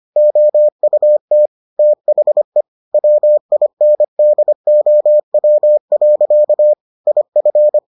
Morse code, heard as a keyed tone, about 25 wpm, 600 Hz. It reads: OUT THE WINDOW. IF